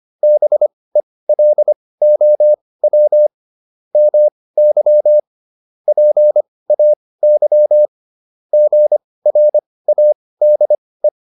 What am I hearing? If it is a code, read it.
Morse code, heard as a keyed tone, about 25 wpm, spaced as for 18 wpm, 600 Hz. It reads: BELOW MY PAY GRADE